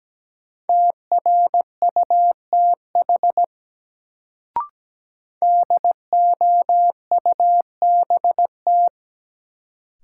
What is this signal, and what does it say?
Morse code, heard as a keyed tone, about 17 wpm, 700 Hz. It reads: TRUTH DOUBT
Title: Sets of 5 Words - New Word Drill - 17wpm